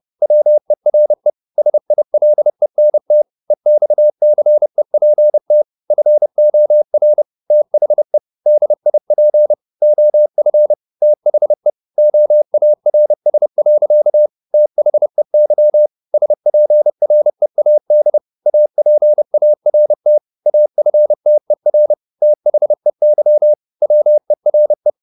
Morse code, heard as a keyed tone, about 30 wpm, 600 Hz. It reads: WERE SILENT EXCEPT FOR THE DIP OF THE OARS. THEY SPREAD APART AFTER THEY WERE
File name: the old man and the sea / the old man and the sea 1 30wpm